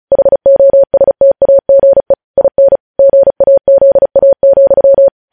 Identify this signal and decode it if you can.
Morse code, heard as a keyed tone, about 36 wpm, 550 Hz. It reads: HOSTAGE IN GAZA,
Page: news headlines in morse code at 35 wpm